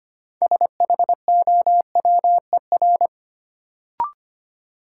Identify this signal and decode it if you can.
Morse code, heard as a keyed tone, about 25 wpm, 700 Hz. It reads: SHOWER
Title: Sentences from Top 1600 Words - New Word Drill Alt - 25wpm